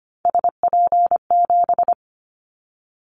Morse code, heard as a keyed tone, about 25 wpm, 700 Hz. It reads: SP7
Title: Callsign Prefixes for Top 25 DXCC - 25wpm